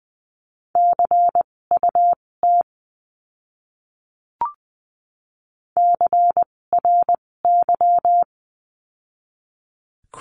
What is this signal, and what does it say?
Morse code, heard as a keyed tone, about 20 wpm, spaced as for 16 wpm, 700 Hz. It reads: CUT CRY
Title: 256 Lesson 42a - Words 12-20wpm